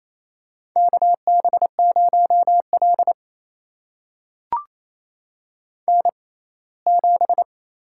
Morse code, heard as a keyed tone, about 28 wpm, spaced as for 19 wpm, 700 Hz. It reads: KB0L N7
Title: USA Call Signs - 4 Characters - 6x Character Spacing - 28wpm